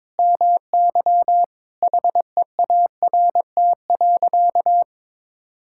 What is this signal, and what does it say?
Morse code, heard as a keyed tone, about 22 wpm, 700 Hz. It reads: MY HEART.
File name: practice / 22wpm / Sentences with 8-Word Phrases - 22wpm